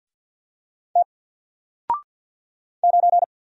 Morse code, heard as a keyed tone, about 48 wpm, 700 Hz. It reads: T 9
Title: Single Letter-Number Alt 50wpm